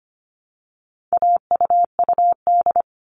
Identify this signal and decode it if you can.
Morse code, heard as a keyed tone, about 25 wpm, 700 Hz. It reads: AUUD
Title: All Cut Numbers - 4-Digit 25wpm